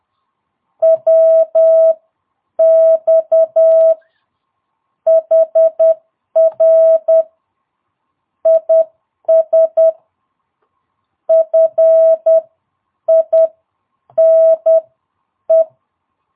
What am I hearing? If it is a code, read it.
Morse code, heard as a keyed tone, about 10 wpm, spaced as for 8 wpm, 650 Hz. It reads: WX HR IS FINE
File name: jh8cbh / 2022cw